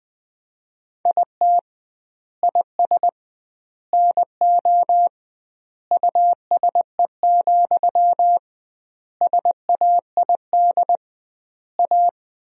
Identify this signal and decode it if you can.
Morse code, heard as a keyed tone, about 20 wpm, 700 Hz. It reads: IT IS NO USE, SAID A
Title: 2x Word-Spacing - Sentences from Top 500 Words - 20wpm